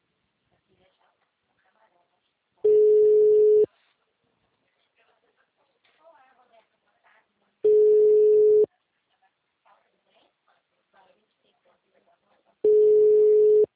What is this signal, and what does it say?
Ringback tone (Europe, 425 Hz), with speech.
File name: out- -2007- - - .64450